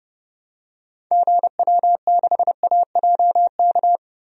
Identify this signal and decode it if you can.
Morse code, heard as a keyed tone, about 32 wpm, 700 Hz. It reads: GW6AJK
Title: DXCC-100 Call Signs - 6 Characters - Repeated 2x - 30wpm